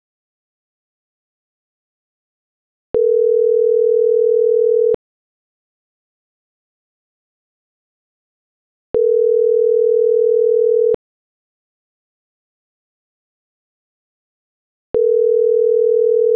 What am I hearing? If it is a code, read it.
Ringback tone (North America), 440 + 480 Hz.